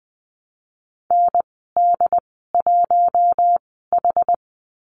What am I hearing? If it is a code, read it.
Morse code, heard as a keyed tone, about 20 wpm, spaced as for 14 wpm, 700 Hz. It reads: ND1H
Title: 069 Lesson 12g - Letter d - Call Signs 20wpm